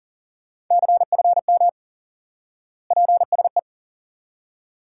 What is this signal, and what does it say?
Morse code, heard as a keyed tone, about 39 wpm, 700 Hz. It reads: CFM PSE
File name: IQE Course 3 - Lesson 89 - YAGI - Two Elements - 40wpm